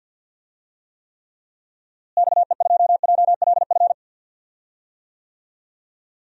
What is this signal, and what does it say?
Morse code, heard as a keyed tone, about 48 wpm, 700 Hz. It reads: XE2JLF
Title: DXCC-100 Call Signs - 6 Characters - Repeated 3x - 50wpm